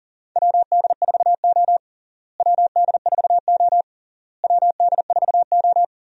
Morse code, heard as a keyed tone, about 40 wpm, 700 Hz. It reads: WD4O WD4O WD4O
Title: USA Call Signs - 4 Characters - Repeated 3x - 40wpm